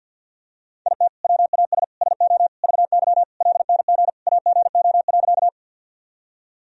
Morse code, heard as a keyed tone, about 46 wpm, 700 Hz. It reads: IT WAS SO VY LNG AGO.
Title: Sentences from Top 500 Words and Abbreviations 50wpm